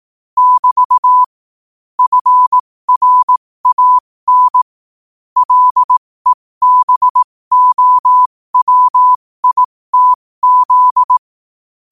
Morse code, heard as a keyed tone, about 18 wpm, spaced as for 15 wpm, 1000 Hz. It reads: <BT> FRAN LEBOWITZ
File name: qotdfiles / 15wpm